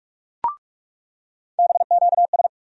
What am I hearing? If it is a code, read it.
Morse code, heard as a keyed tone, about 47 wpm, 700 Hz. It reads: BQS
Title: USA Call Sign Postfixes - 3 Letters 45wpm